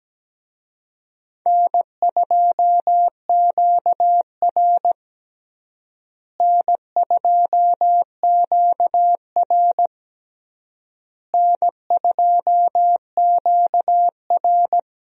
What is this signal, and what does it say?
Morse code, heard as a keyed tone, about 17 wpm, 700 Hz. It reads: N2QR N2QR N2QR
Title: USA Call Signs - 4 Characters - Repeated 3x - 17wpm